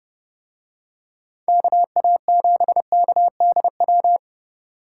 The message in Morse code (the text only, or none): KA7KDW